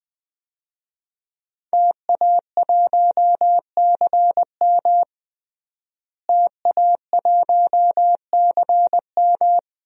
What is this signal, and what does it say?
Morse code, heard as a keyed tone, about 20 wpm, 700 Hz. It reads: TA1CM TA1CM
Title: DXCC-100 Call Signs - 5 Characters - Repeated 2x - 20wpm